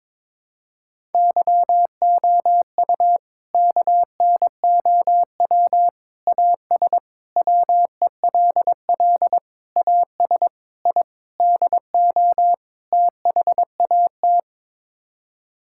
Morse code, heard as a keyed tone, about 22 wpm, 700 Hz. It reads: YOU KNOW AS WELL AS I DO THAT
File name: practice / 22wpm / Sets of 8 Words - 22wpm